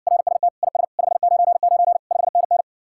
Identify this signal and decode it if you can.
Morse code, heard as a keyed tone, about 60 wpm, 700 Hz. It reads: RST IS 599 5NN